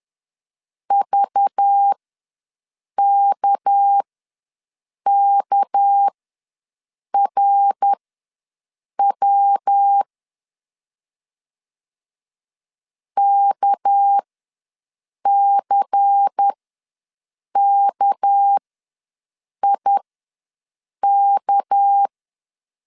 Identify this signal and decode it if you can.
Morse code, heard as a keyed tone, about 10 wpm, spaced as for 6 wpm, 800 Hz. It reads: VKKRW KCKIK
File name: LEZIONI / 6 PKX0 / 6 PK ws 10wpm 3745